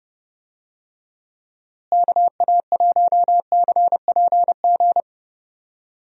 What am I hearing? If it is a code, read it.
Morse code, heard as a keyed tone, about 30 wpm, 700 Hz. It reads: KA1CPG
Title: USA Call Signs - 6 Characters - Repeated 2x - 30wpm